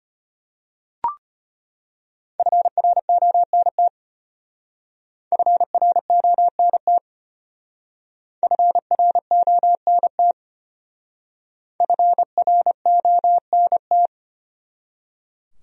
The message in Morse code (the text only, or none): FRONT FRONT FRONT FRONT